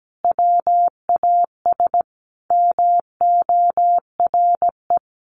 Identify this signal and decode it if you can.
Morse code, heard as a keyed tone, about 17 wpm, 700 Hz. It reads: WAS MORE
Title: Sentences from Top 1200 Words 17wpm